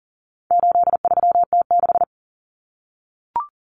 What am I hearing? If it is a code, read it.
Morse code, heard as a keyed tone, about 40 wpm, 700 Hz. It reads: 83T6
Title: Common Cut Numbers - 4-Digit 40wpm